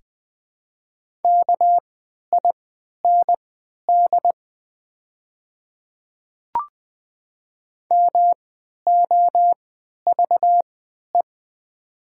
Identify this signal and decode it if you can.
Morse code, heard as a keyed tone, about 21 wpm, spaced as for 12 wpm, 700 Hz. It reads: KIND MOVE
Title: 3x Character Spacing - Top 200 Words - 20wpm